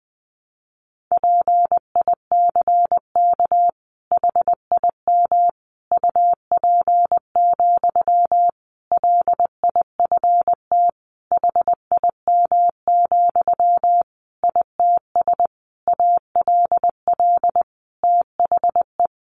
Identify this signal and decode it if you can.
Morse code, heard as a keyed tone, about 20 wpm, 700 Hz. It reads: PICK HIM UP, LIFT HIM, ITS ALL THE
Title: Sentences from Top 1000 Words 20wpm